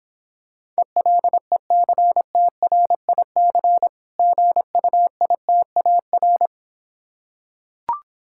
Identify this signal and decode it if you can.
Morse code, heard as a keyed tone, about 26 wpm, 700 Hz. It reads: ELECTRIC GUITAR